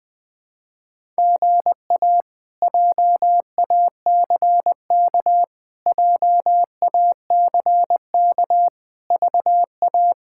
Morse code, heard as a keyed tone, about 20 wpm, 700 Hz. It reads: GA JACK JACK VA
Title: K1USN SST QSO Practice - 20wpm